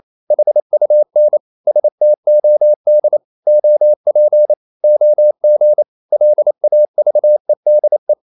Morse code, heard as a keyed tone, about 28 wpm, 600 Hz. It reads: HUN STOD OP OG LAVEDE
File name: Ferie QRQ 28wpm